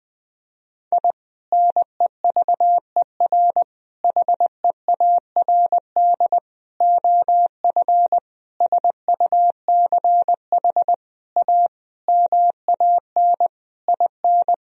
Morse code, heard as a keyed tone, about 20 wpm, 700 Hz. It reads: I NEVER HEARD OF SUCH A MAN IN
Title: Sentences from Top 1000 Words - Review Part 3 - 20wpm